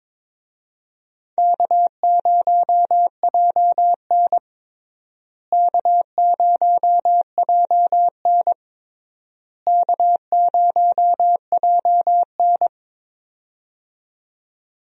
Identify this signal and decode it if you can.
Morse code, heard as a keyed tone, about 21 wpm, 700 Hz. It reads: K0JN K0JN K0JN
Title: USA Call Signs - 4 Characters - Repeated 3x - 22wpm